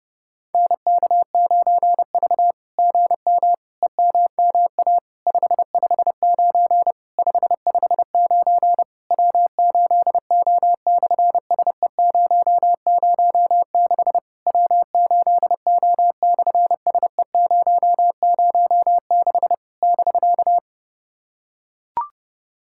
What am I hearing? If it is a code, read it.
Morse code, heard as a keyed tone, about 30 wpm, 700 Hz. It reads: NK9V GM EMMA 559 559 W8O/SE006 W8O/SE006 <BK>
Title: SOTA QSO Practice 30wpm